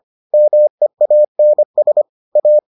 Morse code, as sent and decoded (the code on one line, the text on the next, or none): -- . .- -. ... / .-
MEANS A